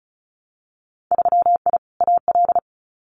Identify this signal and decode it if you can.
Morse code, heard as a keyed tone, about 36 wpm, 700 Hz. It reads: 3I AL
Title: Winter Field Day Exchanges Encore 35wpm